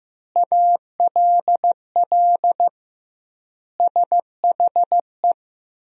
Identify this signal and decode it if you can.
Morse code, heard as a keyed tone, about 15 wpm, 700 Hz. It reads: ALL SHE
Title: 2x Word-Spacing - Sentences from Top 400 Words - 15wpm